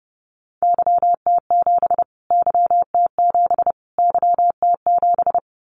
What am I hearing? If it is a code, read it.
Morse code, heard as a keyed tone, about 30 wpm, 700 Hz. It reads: YT7 YT7 YT7
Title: Callsign Prefixes for Top 25 DXCC - Repeated 3x - 30wpm